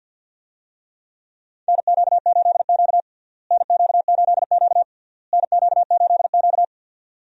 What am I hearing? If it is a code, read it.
Morse code, heard as a keyed tone, about 52 wpm, 700 Hz. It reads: NQ8Q NQ8Q NQ8Q
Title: USA Call Signs - 4 Characters - Repeated 3x - 50wpm